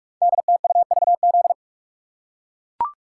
Morse code, heard as a keyed tone, about 44 wpm, 700 Hz. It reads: DTUVZ
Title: Code Groups - 5 Letters - 45wpm